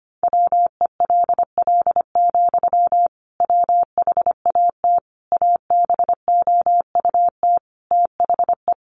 Morse code, heard as a keyed tone, about 25 wpm, 700 Hz. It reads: WELL, WHAT ABOUT THE
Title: Sentences from Top 1200 Words 25wpm